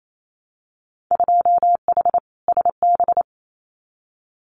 Morse code, heard as a keyed tone, about 28 wpm, 700 Hz. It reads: 2H SB
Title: Winter Field Day Exchanges Encore 28wpm